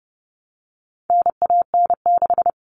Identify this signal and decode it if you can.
Morse code, heard as a keyed tone, about 30 wpm, 700 Hz. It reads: NAN6
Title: All Cut Numbers - 4-Digit 30wpm